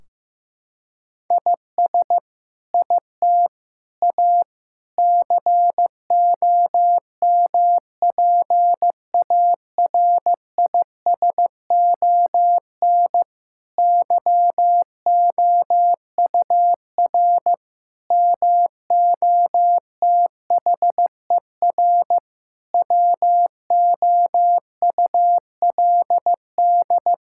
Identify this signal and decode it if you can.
Morse code, heard as a keyed tone, about 15 wpm, 700 Hz. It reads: IS IT A COMPARISON YOUR MOTHER WOULD
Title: Sentences from Top 1600 Words 15wpm